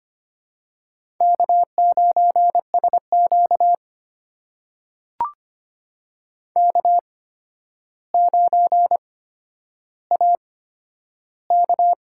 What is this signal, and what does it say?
Morse code, heard as a keyed tone, about 25 wpm, spaced as for 11 wpm, 700 Hz. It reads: K9SQ K9AK
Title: USA Call Signs - 4 Characters - 8x Character Spacing - 25wpm